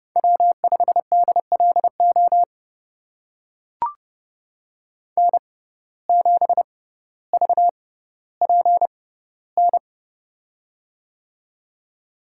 Morse code, heard as a keyed tone, about 29 wpm, spaced as for 15 wpm, 700 Hz. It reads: W5DLO N7VPN
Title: USA Call Signs - 5 Characters - 6x Character Spacing - 30wpm